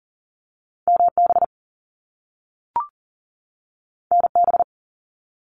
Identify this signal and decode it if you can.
Morse code, heard as a keyed tone, about 40 wpm, 700 Hz. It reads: MB NB
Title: Canadian Provinces and Territories 40wpm